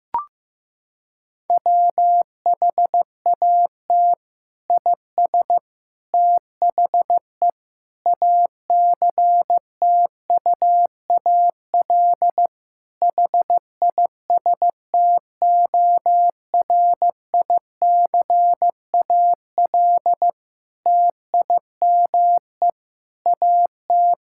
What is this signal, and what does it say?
Morse code, heard as a keyed tone, about 16 wpm, 700 Hz. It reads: WHAT IS THE ACTUAL HISTORICAL TIME AT
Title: Sentences from Top 1000 Words - Review Part 5 - 15wpm